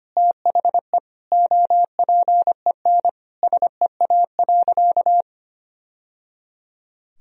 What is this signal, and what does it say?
Morse code, heard as a keyed tone, about 25 wpm, 700 Hz. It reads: THE OPEN SEA.
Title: Sentences from Top 1000 Words - Review Part 1 - 25wpm